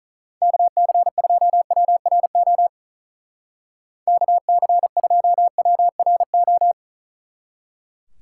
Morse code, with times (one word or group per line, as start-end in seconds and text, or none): 0.4-2.7 KC2WRO
4.1-6.7 KC2WRO